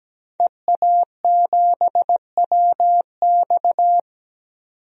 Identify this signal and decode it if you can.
Morse code, heard as a keyed tone, about 17 wpm, 700 Hz. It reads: EA7WX
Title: DXCC-100 Call Signs - 5 Characters - 17wpm